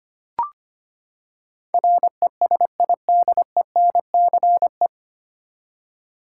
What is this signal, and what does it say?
Morse code, heard as a keyed tone, about 25 wpm, 700 Hz. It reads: RESIDENCE
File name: Nine Letter Words - 25wpm